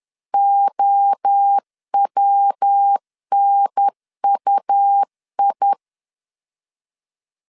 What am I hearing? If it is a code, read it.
Morse code, heard as a keyed tone, about 10 wpm, 800 Hz. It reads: OWNUI
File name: CW / DET / 1 3 1500C S B 10wpm 3335